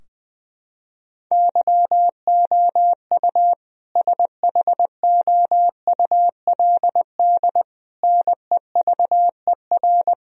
Morse code, heard as a keyed tone, about 20 wpm, 700 Hz. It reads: YOU SHOULD NEVER
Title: Sentences from Top 1000 Words - Review Part 3 - 20wpm